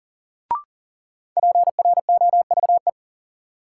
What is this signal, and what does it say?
Morse code, heard as a keyed tone, about 39 wpm, 700 Hz. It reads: PROVE
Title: Sentences from Top 1600 Words - New Word Drill - 40wpm